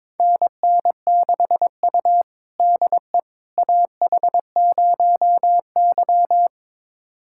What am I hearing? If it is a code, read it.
Morse code, heard as a keyed tone, about 22 wpm, 700 Hz. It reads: NN6U DE AH0Y
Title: USA Callsign DE Callsign - 4 Characters - 22wpm